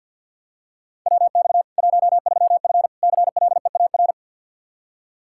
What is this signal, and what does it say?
Morse code, heard as a keyed tone, about 47 wpm, 700 Hz. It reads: WX 13F CLEAR
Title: K1USN SST QSO Practice 50wpm